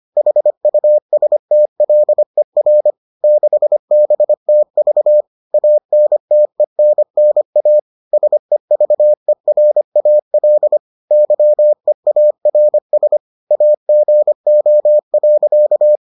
Morse code, heard as a keyed tone, about 25 wpm, 600 Hz. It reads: HUSTLER 6BTV ANTENNA SEVERAL YEARS AGO.